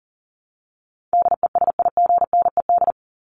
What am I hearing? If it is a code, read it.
Morse code, heard as a keyed tone, about 40 wpm, 700 Hz. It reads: DESIGNED
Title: ED Suffix - Top 100 Words with Base Words 40wpm